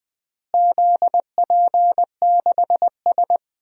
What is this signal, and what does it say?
Morse code, heard as a keyed tone, about 20 wpm, 700 Hz. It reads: ZP6S
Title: DXCC-100 Call Signs - 4 Characters - Repeated 3x - 20wpm